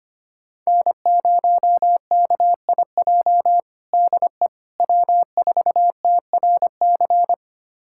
Morse code, heard as a keyed tone, about 25 wpm, 700 Hz. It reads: N0KIJ DE W4TRC
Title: USA Callsign DE Callsign - 5 Characters - 25wpm